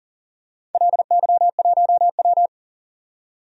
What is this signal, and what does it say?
Morse code, heard as a keyed tone, about 40 wpm, 700 Hz. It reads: LY1W
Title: DXCC-100 Call Signs - 4 Characters - 40wpm